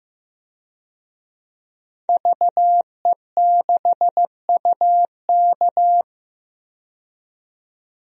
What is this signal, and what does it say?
Morse code, heard as a keyed tone, about 15 wpm, 700 Hz. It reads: VE6UK
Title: DXCC-100 Call Signs - 5 Characters - 15wpm